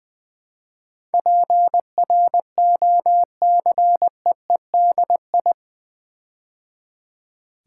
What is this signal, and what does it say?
Morse code, heard as a keyed tone, about 20 wpm, 700 Hz. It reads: PROCEEDI
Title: Build A Word - 4 Letters At A Time - 10-Letter Words - 20wpm